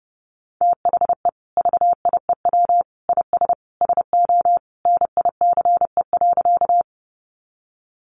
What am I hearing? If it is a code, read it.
Morse code, heard as a keyed tone, about 30 wpm, 700 Hz. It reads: THE VIEW IS SO NICE.
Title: Additional Easy Phrases 30wpm